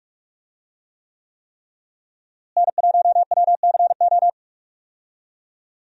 Morse code, heard as a keyed tone, about 45 wpm, 700 Hz. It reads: N1WCO
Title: USA Call Signs - 5 Characters - Repeated 3x - 45wpm